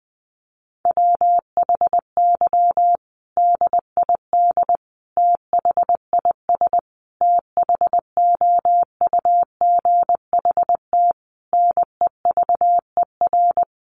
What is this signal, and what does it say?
Morse code, heard as a keyed tone, about 20 wpm, 700 Hz. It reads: WHY DID THIS THOUGHT NEVER
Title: Sentences from Top 900 Words 20wpm